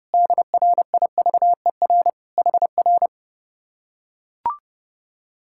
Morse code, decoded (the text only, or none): DRIVER HR